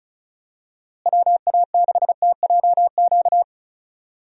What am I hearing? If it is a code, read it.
Morse code, heard as a keyed tone, about 36 wpm, 700 Hz. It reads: WA6TJQ
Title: USA Call Signs - 6 Characters - Repeated 3x - 35wpm